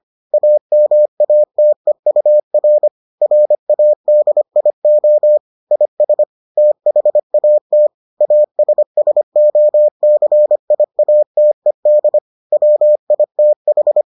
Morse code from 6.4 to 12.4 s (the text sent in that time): THAT ASSOCIATED